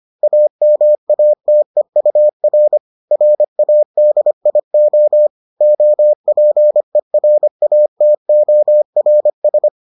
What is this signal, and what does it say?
Morse code, heard as a keyed tone, about 25 wpm, 600 Hz. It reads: AMATEUR RADIO OPERATORS